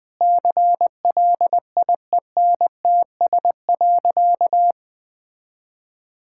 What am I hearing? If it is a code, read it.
Morse code, heard as a keyed tone, about 20 wpm, 700 Hz. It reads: CLIENTS.
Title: Sentences with 5-Word Phrases - 20wpm